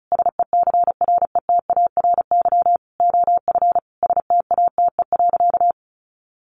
Morse code, heard as a keyed tone, about 34 wpm, 700 Hz. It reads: SECRETARY OF STATE.
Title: Sentences from Top 1400 Words 35wpm